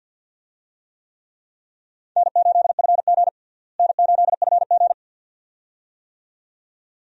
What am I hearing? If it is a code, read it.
Morse code, heard as a keyed tone, about 51 wpm, 700 Hz. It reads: N8FG N8FG
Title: USA Call Signs - 4 Characters - Repeated 2x - 50wpm